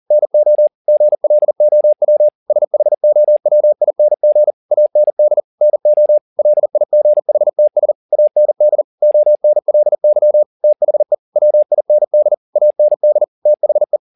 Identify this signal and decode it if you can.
Morse code, heard as a keyed tone, about 40 wpm, 600 Hz. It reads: NO GLOW SHOWING AND NO LIGHTS AND ONLY THE WIND AND THE
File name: the old man and the sea 2 40wpm